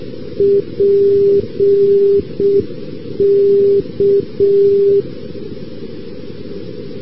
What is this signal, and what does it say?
Morse code, heard as a keyed tone, about 6 wpm, 400 Hz. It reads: PK
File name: europe / rus / rus-2189 ndb st. petersburg-pulkovo pk-342-np-fin-maakeskidx- a-rb